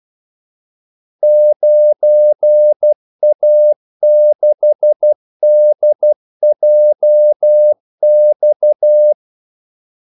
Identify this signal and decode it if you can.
Morse code, heard as a keyed tone, about 12 wpm, 600 Hz. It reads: 9A6DJX
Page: 2 ekstra ordmellemrum mellem kaldesignalerne